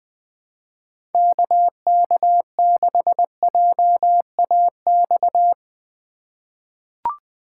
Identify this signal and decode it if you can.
Morse code, heard as a keyed tone, about 20 wpm, 700 Hz. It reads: KK6JAX
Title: USA Call Signs - 2x Character Spacing - 20wpm